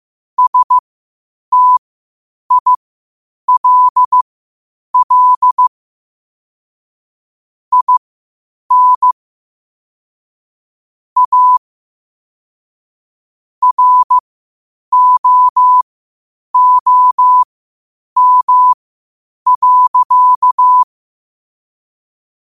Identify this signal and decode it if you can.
Morse code, heard as a keyed tone, about 15 wpm, spaced as for 8 wpm, 1000 Hz. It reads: STILL IN A ROOM.
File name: qotdfiles / 8wpm